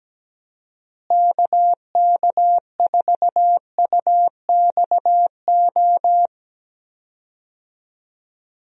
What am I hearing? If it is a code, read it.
Morse code, heard as a keyed tone, about 17 wpm, 700 Hz. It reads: KK4UXO
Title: USA Call Signs - 6 Characters - Repeated 3x - 17wpm